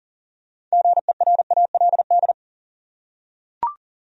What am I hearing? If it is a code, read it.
Morse code, heard as a keyed tone, about 38 wpm, 700 Hz. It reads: GERALD
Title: USA Common Names - 40wpm